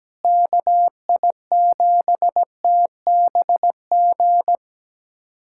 Morse code, heard as a keyed tone, about 17 wpm, 700 Hz. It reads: KI7TBG
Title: USA Callsign DE Callsign - 6 Characters - 17wpm